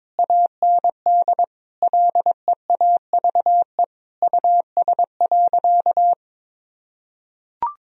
Morse code, heard as a keyed tone, about 22 wpm, 700 Hz. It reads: AND LEAVE US.
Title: Sentences from Top 1000 Words - Review Part 3 - 22wpm